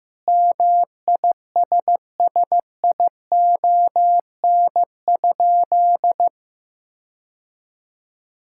Morse code, heard as a keyed tone, about 14 wpm, 700 Hz. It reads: MISSION?
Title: Sentences from Top 1600 Words 15wpm